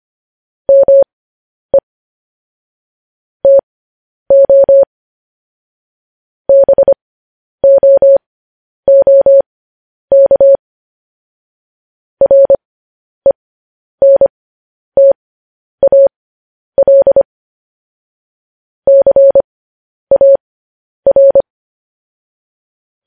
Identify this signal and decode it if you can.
Morse code, heard as a keyed tone, about 25 wpm, spaced as for 10 wpm, 550 Hz. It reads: ME TO BOOK RENTAL CAR
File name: podcast / News headlines-1x-25x10